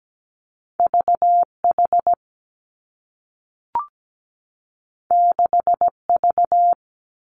Morse code, heard as a keyed tone, about 17 wpm, 700 Hz. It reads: VH 6V
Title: Two Challenging Letter-Number Combinations 17wpm